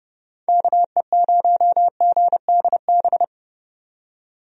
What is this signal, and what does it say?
Morse code, heard as a keyed tone, about 30 wpm, 700 Hz. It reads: KE0GDB
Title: USA Call Signs - 6 Characters - 30wpm